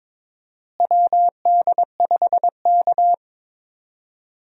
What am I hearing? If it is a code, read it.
Morse code, heard as a keyed tone, about 21 wpm, 700 Hz. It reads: WD5K
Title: USA Call Signs - 4 Characters - Repeated 2x - 22wpm